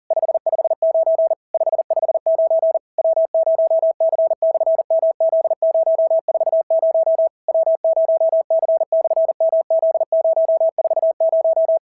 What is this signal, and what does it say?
Morse code, heard as a keyed tone, about 40 wpm, 650 Hz. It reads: 559 559 W0C/MZ040 W0C/MZ040
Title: SOTA QSO Practice 40wpm